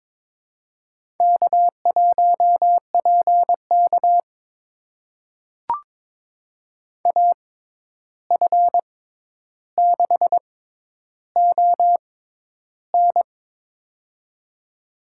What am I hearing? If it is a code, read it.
Morse code, heard as a keyed tone, about 22 wpm, spaced as for 11 wpm, 700 Hz. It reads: K1PK AF6ON
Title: USA Call Signs - 6x Character Spacing - 22wpm